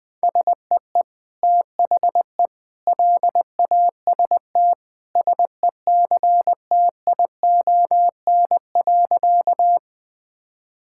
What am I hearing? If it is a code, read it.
Morse code, heard as a keyed tone, about 20 wpm, 700 Hz. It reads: SEE THE LAST SECTION.
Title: Sentences from Top 1000 Words - Review Part 5 - 20wpm